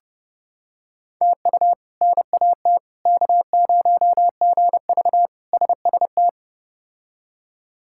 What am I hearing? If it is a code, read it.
Morse code, heard as a keyed tone, about 32 wpm, 700 Hz. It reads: TU NAT K0GV SST
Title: K1USN SST QSO Practice - 30wpm